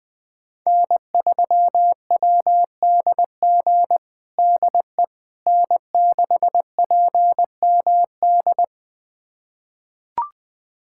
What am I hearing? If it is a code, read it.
Morse code, heard as a keyed tone, about 20 wpm, 700 Hz. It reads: N3WDG DE N6PMD
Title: USA Callsign DE Callsign - 5 Characters - 20wpm